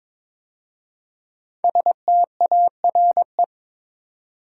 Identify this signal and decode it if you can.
Morse code, heard as a keyed tone, about 22 wpm, 700 Hz. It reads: STARE
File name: 5 Letter Words - First 3 Letters Given - 22wpm